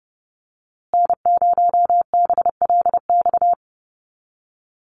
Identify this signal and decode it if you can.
Morse code, heard as a keyed tone, about 30 wpm, 700 Hz. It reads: N0BLX
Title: USA Call Signs Repeated 2x 30wpm